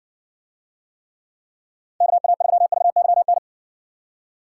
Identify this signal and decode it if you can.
Morse code, heard as a keyed tone, about 60 wpm, 700 Hz. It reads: KA3VYN